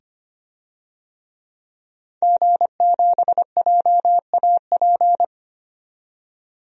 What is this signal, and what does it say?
Morse code, heard as a keyed tone, about 25 wpm, 700 Hz. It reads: G7JAP
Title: DXCC-100 Call Signs - 5 Characters - 25wpm